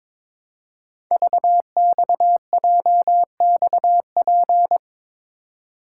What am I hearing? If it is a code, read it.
Morse code, heard as a keyed tone, about 22 wpm, 700 Hz. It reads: VXJXP
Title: Code Groups - 5 Letter:Numbers - 22wpm